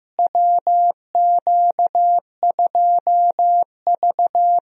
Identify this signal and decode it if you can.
Morse code, heard as a keyed tone, about 15 wpm, 700 Hz. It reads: WQ2V
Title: USA Call Signs - 4 Characters - Repeated 2x - 15wpm